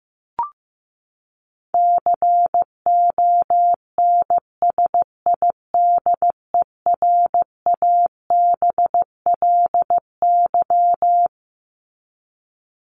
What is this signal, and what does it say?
Morse code, heard as a keyed tone, about 15 wpm, 700 Hz. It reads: CONSIDERABLY
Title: LY Suffix - Top 100 Words with Base Words 15wpm